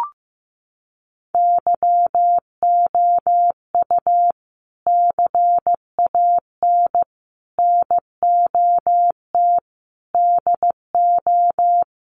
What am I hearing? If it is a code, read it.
Morse code, heard as a keyed tone, about 15 wpm, 700 Hz. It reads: YOU CAN NOT DO
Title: Sets of 5 Words 15wpm